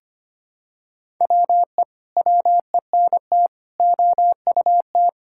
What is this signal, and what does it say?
Morse code, heard as a keyed tone, about 25 wpm, 700 Hz. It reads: WE WENT OUT
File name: practice / 25wpm / Sentences with 9-Word Phrases - 25wpm